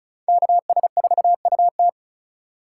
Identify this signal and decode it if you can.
Morse code, heard as a keyed tone, about 36 wpm, 700 Hz. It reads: KS4UT
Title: USA Call Signs - 5 Characters - 2x Character Spacing - 35wpm